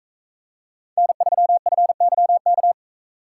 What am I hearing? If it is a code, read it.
Morse code, heard as a keyed tone, about 42 wpm, 700 Hz. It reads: N3FYK